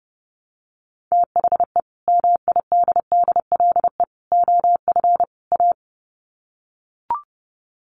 Morse code, heard as a keyed tone, about 30 wpm, 700 Hz. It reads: THE MIDDLE OF A
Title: Sets of 4 Words 30wpm